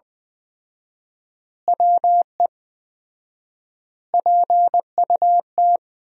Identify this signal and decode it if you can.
Morse code, heard as a keyed tone, about 20 wpm, 700 Hz. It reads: WE PUT
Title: 4x Word-Spacing - Sentences from Top 200 Words - 20wpm